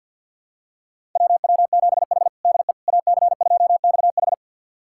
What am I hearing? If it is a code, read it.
Morse code, heard as a keyed tone, about 51 wpm, 700 Hz. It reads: WW7H DE AC2XH
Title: USA Callsign DE Callsign 50wpm